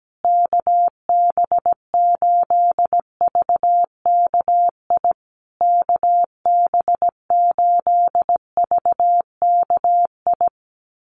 Morse code, heard as a keyed tone, about 17 wpm, 700 Hz. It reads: KB8VKI KB8VKI
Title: USA Call Signs Repeated 3x 17wpm